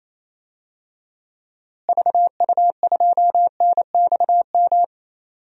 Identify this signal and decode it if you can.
Morse code, heard as a keyed tone, about 28 wpm, 700 Hz. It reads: VU2NXM
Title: DXCC-100 Call Signs - 6 Characters - Repeated 3x - 28wpm